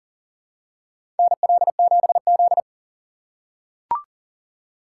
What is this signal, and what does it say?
Morse code, heard as a keyed tone, about 40 wpm, 700 Hz. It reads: NL7Z
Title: USA Call Signs - 2x Character Spacing - 40wpm